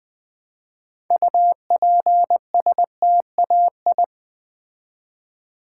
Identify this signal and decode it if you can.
Morse code, heard as a keyed tone, about 20 wpm, 700 Hz. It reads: UPSTAI
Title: Build A Word - 3 Letters At A Time - 8-Letter Words - 20wpm